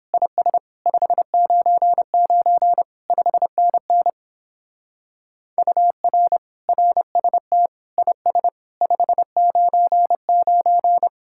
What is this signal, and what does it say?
Morse code, heard as a keyed tone, about 26 wpm, 700 Hz. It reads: IS 599 5NN UR RST IS 599
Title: Speed-Racing Common QSO Phrases Encore - 22wpm